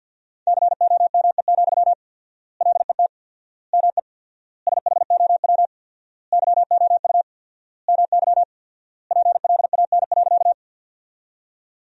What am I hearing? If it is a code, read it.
Morse code, heard as a keyed tone, about 50 wpm, 700 Hz. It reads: COME, LET ME SHOW YOU MY PLAN.
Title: 4x Word-Spacing - Sentences from Top 500 Words - 50wpm